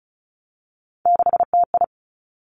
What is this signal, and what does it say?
Morse code, heard as a keyed tone, about 35 wpm, 700 Hz. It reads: 6TI
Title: Three Challenging Letter-Number Combinations 35wpm